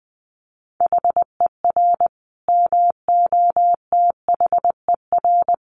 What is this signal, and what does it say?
Morse code, heard as a keyed tone, about 20 wpm, 700 Hz. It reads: HER MOTHER
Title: Sentences from Top 1300 Words 20wpm